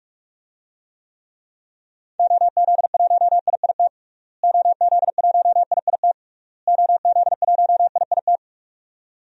Morse code, heard as a keyed tone, about 46 wpm, 700 Hz. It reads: OZ1IIT OZ1IIT OZ1IIT
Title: DXCC-100 Call Signs - 6 Characters - Repeated 3x - 45wpm